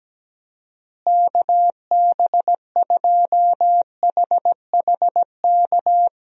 Morse code, heard as a keyed tone, about 17 wpm, 700 Hz. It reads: KB2HHK
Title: USA Call Signs - 6 Characters - Repeated 2x - 17wpm